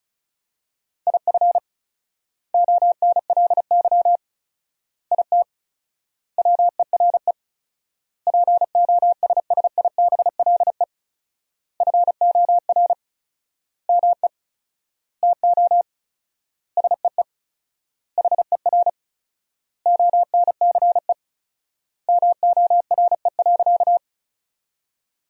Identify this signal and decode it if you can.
Morse code, heard as a keyed tone, about 36 wpm, 700 Hz. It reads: IF ONLY IT WERE POSSIBLE FOR ME TO SEE HER ONCE MORE.
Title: 4x Word-Spacing - Sentences from Top 500 Words - 35wpm